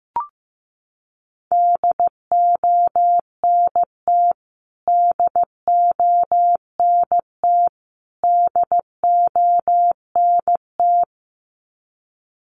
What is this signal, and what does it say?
Morse code, heard as a keyed tone, about 15 wpm, 700 Hz. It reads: DONT DONT DONT
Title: Top 300 Words Repeated 15wpm